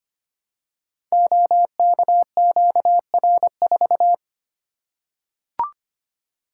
Morse code, heard as a keyed tone, about 25 wpm, 700 Hz. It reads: OKQR4
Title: Code Groups - 5 Letter:Numbers - 25wpm